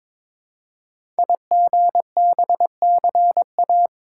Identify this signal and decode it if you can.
Morse code, heard as a keyed tone, about 22 wpm, 700 Hz. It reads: IGBCA
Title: Code Groups - 5 Letters - 22wpm